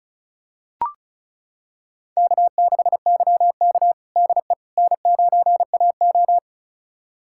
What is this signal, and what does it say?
Morse code, heard as a keyed tone, about 34 wpm, 690 Hz. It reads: K6YK DE N9AO
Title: USA Callsign DE Callsign - 4 Characters - 35wpm